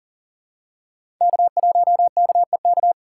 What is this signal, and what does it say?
Morse code, heard as a keyed tone, about 40 wpm, 700 Hz. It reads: K1KEK
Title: USA Call Signs - 5 Characters - 3x Character Spacing - 40wpm